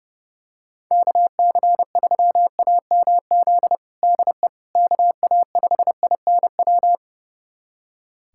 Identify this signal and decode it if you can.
Morse code, heard as a keyed tone, about 30 wpm, 700 Hz. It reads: KC3AMZ DE KA5INW
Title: USA Callsign DE Callsign - 6 Characters - 30wpm